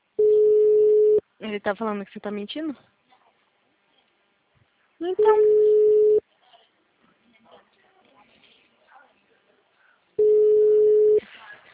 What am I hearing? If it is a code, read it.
Ringback tone (Europe, 425 Hz), with speech.